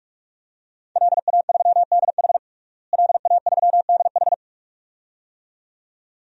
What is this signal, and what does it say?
Morse code, heard as a keyed tone, about 46 wpm, 700 Hz. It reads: LA3DH LA3DH
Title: DXCC-100 Call Signs - 5 Characters - Repeated 2x - 45wpm